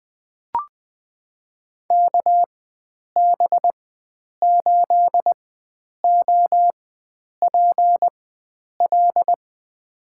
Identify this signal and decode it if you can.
Morse code, heard as a keyed tone, about 20 wpm, spaced as for 9 wpm, 700 Hz. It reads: KB8OPL
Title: USA Call Signs - 6 Characters - 4x Character Spacing - 20wpm